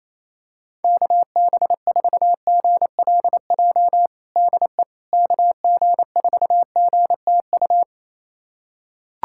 Morse code, heard as a keyed tone, about 28 wpm, 700 Hz. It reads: KB4GLJ DE KG4GTU
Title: USA Callsign DE Callsign - 6 Characters - 28wpm